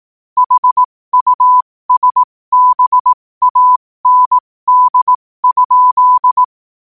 Morse code, heard as a keyed tone, about 18 wpm, spaced as for 15 wpm, 1000 Hz. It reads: HUSBAND?
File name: 15wpm